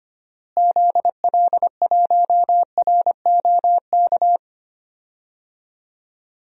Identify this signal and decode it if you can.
Morse code, heard as a keyed tone, about 25 wpm, 700 Hz. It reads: ZL1ROK
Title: DXCC-100 Call Signs - 6 Characters - Repeated 3x - 25wpm